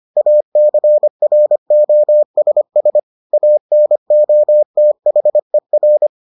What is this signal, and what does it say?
Morse code, heard as a keyed tone, about 25 wpm, 600 Hz. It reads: ACROSS ANOTHER